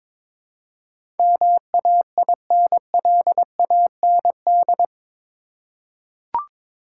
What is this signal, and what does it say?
Morse code, heard as a keyed tone, about 22 wpm, 700 Hz. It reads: MAINLAND
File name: practice / 22wpm / Eight Letter Words - 22wpm